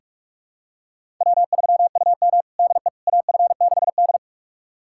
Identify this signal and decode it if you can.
Morse code, heard as a keyed tone, about 46 wpm, 700 Hz. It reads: W3UM DE AF6D
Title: USA Callsign DE Callsign - 4 Characters - 45wpm